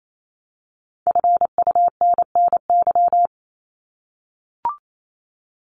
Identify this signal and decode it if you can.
Morse code, heard as a keyed tone, about 28 wpm, 700 Hz. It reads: FUNNY
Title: Top 1751-2000 Words 28wpm